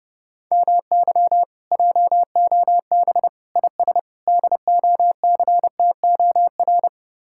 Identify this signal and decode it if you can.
Morse code, heard as a keyed tone, about 30 wpm, 700 Hz. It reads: MY JOB IS DOCTOR